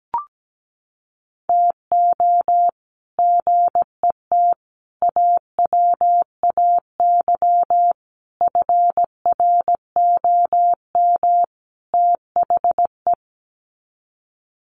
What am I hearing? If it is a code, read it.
Morse code, heard as a keyed tone, about 17 wpm, 700 Hz. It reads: TO GET AWAY FROM THE
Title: Sets of 5 Words Encore2 17wpm